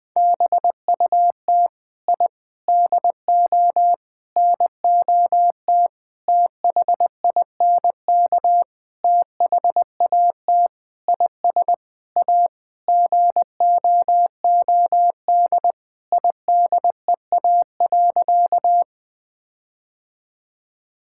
Morse code, heard as a keyed tone, about 20 wpm, 700 Hz. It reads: BUT I DO NOT THINK THAT IS A GOOD IDEA.
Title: Sentences with 5-Word Phrases - 20wpm